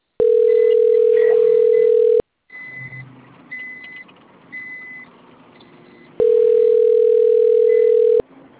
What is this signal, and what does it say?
Ringback tone (North America), 440 + 480 Hz.
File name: rg-1600-unknown- - - .48952